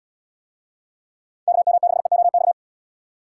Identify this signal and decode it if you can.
Morse code, heard as a keyed tone, about 73 wpm, 700 Hz. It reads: KM6EOX